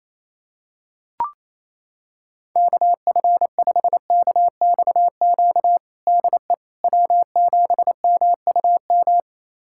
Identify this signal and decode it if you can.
Morse code, heard as a keyed tone, about 28 wpm, 700 Hz. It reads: KF5KXQ DE W7MUM
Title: USA Callsign DE Callsign 28wpm